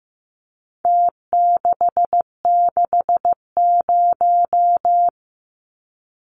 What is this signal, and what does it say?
Morse code, heard as a keyed tone, about 15 wpm, 700 Hz. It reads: T660
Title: Numbers - Contest Style - 4-Digit 15wpm